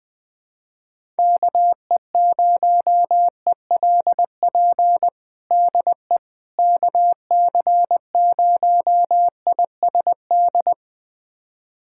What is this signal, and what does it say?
Morse code, heard as a keyed tone, about 20 wpm, 700 Hz. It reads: KE0ELP DE KC0ISD
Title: USA Callsign DE Callsign - 6 Characters - 20wpm